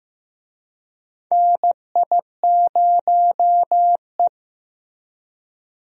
Morse code, heard as a keyed tone, about 16 wpm, 700 Hz. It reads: NI0E
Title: USA Call Signs - 4 Characters - Repeated 2x - 15wpm